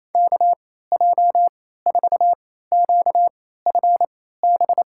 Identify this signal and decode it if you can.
Morse code, heard as a keyed tone, about 27 wpm, spaced as for 16 wpm, 700 Hz. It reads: KJ4QFB
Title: USA Call Signs - 6 Characters - 3x Character Spacing - 28wpm